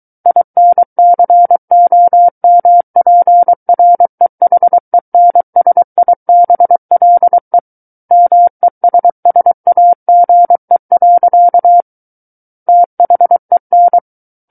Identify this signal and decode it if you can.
Morse code, heard as a keyed tone, about 23 wpm, 700 Hz. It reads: INCOMPREHENSIBLE MESSAGE. THEN
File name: Book One / 14-Chapter 14 - 23 WPM